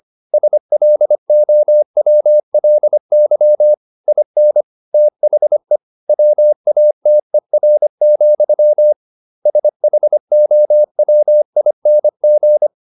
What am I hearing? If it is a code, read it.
Morse code, heard as a keyed tone, about 25 wpm, 600 Hz. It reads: SLOWLY IN THE WATER, SHOWING